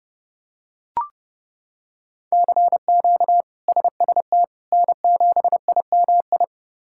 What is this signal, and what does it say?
Morse code, heard as a keyed tone, about 30 wpm, 700 Hz. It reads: CQ SST N7IMI
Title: K1USN SST QSO Practice - 30wpm